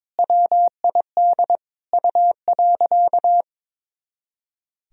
Morse code, heard as a keyed tone, about 22 wpm, 700 Hz. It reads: WID U.
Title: Sentences from Top 300 Words and Abbreviations 22wpm